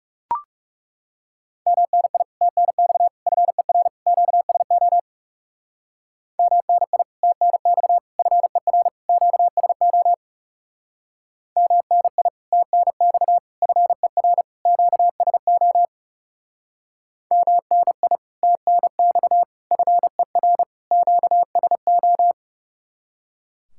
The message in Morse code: MNI TNX FER QSO MNI TNX FER QSO MNI TNX FER QSO MNI TNX FER QSO

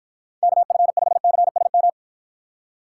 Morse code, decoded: KF5CSM

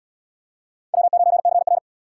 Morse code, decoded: W9CR

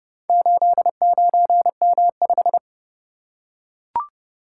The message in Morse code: 89M5

